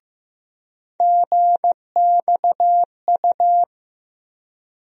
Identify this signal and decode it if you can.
Morse code, heard as a keyed tone, about 16 wpm, 700 Hz. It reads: GXU
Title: USA Call Sign Postfixes - 3 Letters 15wpm